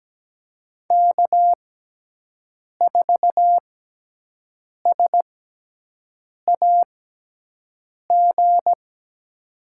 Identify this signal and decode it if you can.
Morse code, heard as a keyed tone, about 17 wpm, spaced as for 6 wpm, 700 Hz. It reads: K4SAG
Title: USA Call Signs - 5 Characters - 6x Character Spacing - 17wpm